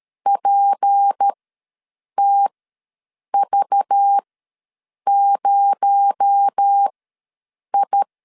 Morse code, heard as a keyed tone, about 13 wpm, spaced as for 7 wpm, 800 Hz. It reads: PTV0I
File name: LEZIONI / 7 GF9Z / 7 G ws 12wpm 3140